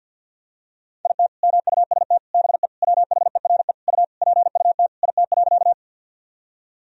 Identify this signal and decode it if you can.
Morse code, heard as a keyed tone, about 46 wpm, 700 Hz. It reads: IT MUST BE WHERE U PUT IT.
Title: Sentences from Top 200 Words and Abbreviations 50wpm